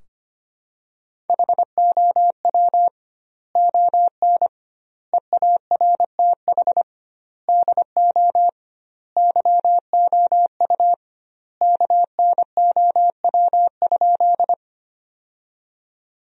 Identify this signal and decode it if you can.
Morse code, heard as a keyed tone, about 25 wpm, 700 Hz. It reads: HOW ON EARTH DO YOU KNOW?
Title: 2x Word-Spacing - Sentences from Top 200 Words - 25wpm